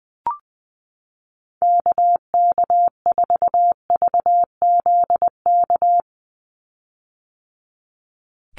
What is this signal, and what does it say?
Morse code, heard as a keyed tone, about 20 wpm, 700 Hz. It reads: KK4VZK
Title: USA Call Signs 20wpm